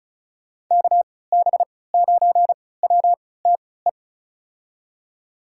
Morse code, heard as a keyed tone, about 36 wpm, spaced as for 20 wpm, 700 Hz. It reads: KB9WTE
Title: USA Call Signs - 6 Characters - 3x Character Spacing - 35wpm